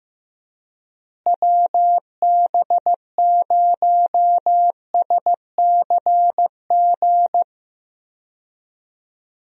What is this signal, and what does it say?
Morse code, heard as a keyed tone, about 16 wpm, 700 Hz. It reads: WB0SCG
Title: USA Call Signs - 6 Characters - Repeated 2x - 15wpm